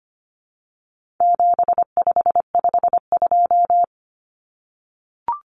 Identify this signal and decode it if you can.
Morse code, heard as a keyed tone, about 25 wpm, 700 Hz. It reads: 7552
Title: Common Cut Numbers - 4-Digit 25wpm